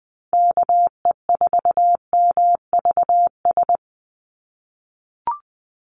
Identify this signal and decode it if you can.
Morse code, heard as a keyed tone, about 20 wpm, 700 Hz. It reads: KE4MVS
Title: USA Call Signs 20wpm